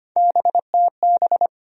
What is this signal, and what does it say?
Morse code, heard as a keyed tone, about 25 wpm, 700 Hz. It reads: BTB
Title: USA Call Sign Postfixes - 3 Letters 25wpm